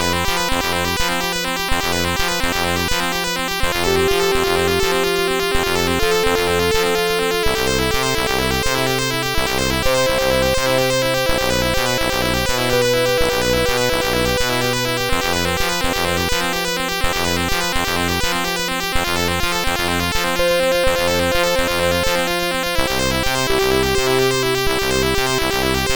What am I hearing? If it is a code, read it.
There is no code.